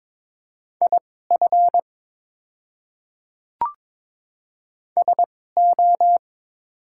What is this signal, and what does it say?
Morse code, heard as a keyed tone, about 22 wpm, spaced as for 16 wpm, 700 Hz. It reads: IF SO
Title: 2x Character Spacing - Top 200 Words - 22wpm